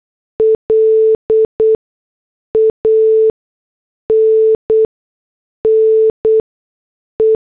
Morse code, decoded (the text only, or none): LANNE